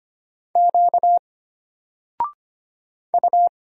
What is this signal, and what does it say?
Morse code, heard as a keyed tone, about 25 wpm, 700 Hz. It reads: Q U
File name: Single-Letter-Number-25wpm